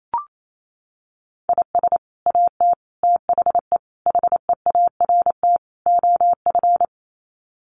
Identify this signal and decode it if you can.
Morse code, heard as a keyed tone, about 28 wpm, 700 Hz. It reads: IS AT THE HEART OF
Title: Sets of 5 Words Encore2 28wpm